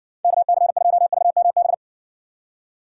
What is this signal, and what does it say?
Morse code, heard as a keyed tone, about 60 wpm, 700 Hz. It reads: KC2VMB